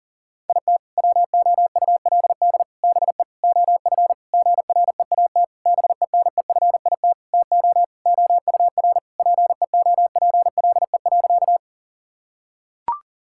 Morse code, heard as a keyed tone, about 40 wpm, 700 Hz. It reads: IT WOULD BE OF GREAT BENEFIT TO OUR PEOPLE.
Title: Sentences from Top 1000 Words - Review Part 6 - 40wpm